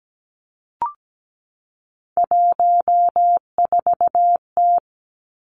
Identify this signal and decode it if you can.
Morse code, heard as a keyed tone, about 17 wpm, 700 Hz. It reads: 14T
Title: Three Challenging Letter-Number Combinations 17wpm